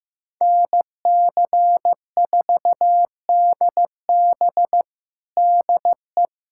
Morse code, heard as a keyed tone, about 15 wpm, 700 Hz. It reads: NC4DB DE
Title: USA Callsign DE Callsign 15wpm